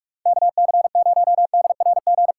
Morse code, heard as a keyed tone, about 46 wpm, 700 Hz. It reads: KC0DRG
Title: USA Callsign DE Callsign - 6 Characters - 45wpm